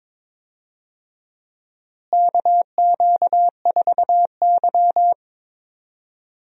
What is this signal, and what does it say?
Morse code, heard as a keyed tone, about 22 wpm, 700 Hz. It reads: KQ4Y
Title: USA Call Signs - 4 Characters - Repeated 2x - 22wpm